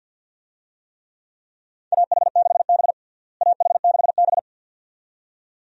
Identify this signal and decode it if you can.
Morse code, heard as a keyed tone, about 51 wpm, 700 Hz. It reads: AH6B AH6B
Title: USA Call Signs - 4 Characters - Repeated 2x - 50wpm